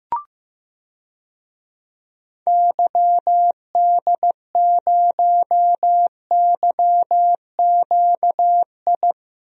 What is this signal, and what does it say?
Morse code, heard as a keyed tone, about 15 wpm, 700 Hz. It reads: YD0YQI
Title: DXCC-100 Call Signs - 6 Characters - Repeated 2x - 15wpm